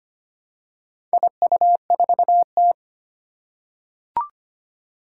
Morse code, heard as a keyed tone, about 25 wpm, 700 Hz. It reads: IU4T
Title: Four Challenging Letter-Number Combinations 25wpm